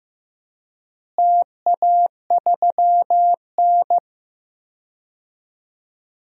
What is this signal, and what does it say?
Morse code, heard as a keyed tone, about 15 wpm, 700 Hz. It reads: TA3N
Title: DXCC-100 Call Signs - 4 Characters - Repeated 2x - 15wpm